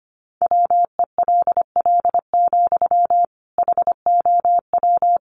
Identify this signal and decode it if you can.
Morse code, heard as a keyed tone, about 25 wpm, 700 Hz. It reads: WELL, HOW
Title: Sentences from Top 1100 Words 25wpm